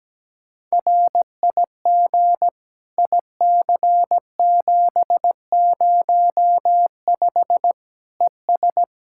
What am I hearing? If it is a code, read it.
Morse code, heard as a keyed tone, about 17 wpm, 700 Hz. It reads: RIG IC705 ES
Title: Practice QSO - Exchange 2 - Info - 17wpm